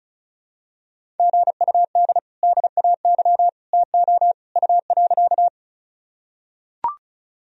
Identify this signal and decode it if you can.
Morse code, heard as a keyed tone, about 34 wpm, 700 Hz. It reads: GUD DAY TO U.
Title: Sentences from Top 200 Words and Abbreviations 35wpm